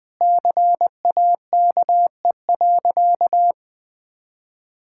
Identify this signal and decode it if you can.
Morse code, heard as a keyed tone, about 20 wpm, 700 Hz. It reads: CAKE.
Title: Sentences with 3-Word Phrases - Spoken Beginning - 20wpm